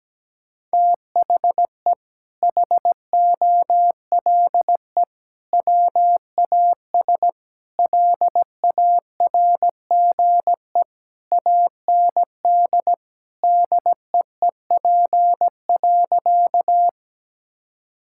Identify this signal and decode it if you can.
Morse code, heard as a keyed tone, about 17 wpm, 700 Hz. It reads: THE HOLE WAS LARGE AND DEEP.
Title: Sentences from Top 1600 Words 17wpm